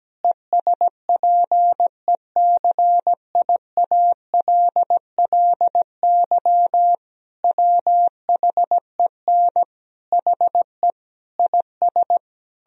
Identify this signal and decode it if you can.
Morse code, heard as a keyed tone, about 17 wpm, 700 Hz. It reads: ESPECIALLY WHEN HE IS
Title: Sentences from Top 1000 Words - Review Part 4 - 17wpm